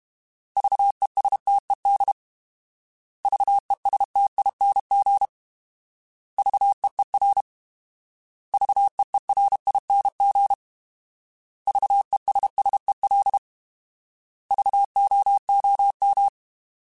Morse code, heard as a keyed tone, about 32 wpm, 800 Hz. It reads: VESTED VESTING VEER VEERING VESSEL VOOM
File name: Alphabetical / Alphabetical mp3s / Vs @30